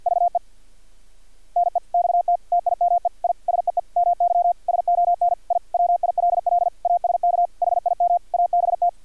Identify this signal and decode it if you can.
Morse code, heard as a keyed tone, about 50 wpm, 700 Hz. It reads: VE NEXT TIME I SEE MY SON I WILL ASK HIM ABT
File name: archived / CRSnet